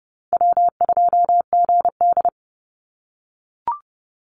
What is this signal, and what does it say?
Morse code, heard as a keyed tone, about 30 wpm, 700 Hz. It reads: W2GD
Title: Easy Call Signs - 30wpm